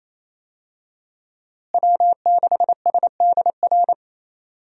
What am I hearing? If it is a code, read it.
Morse code, heard as a keyed tone, about 28 wpm, 700 Hz. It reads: W6SDR
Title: USA Call Signs - 5 Characters - Repeated 2x - 28wpm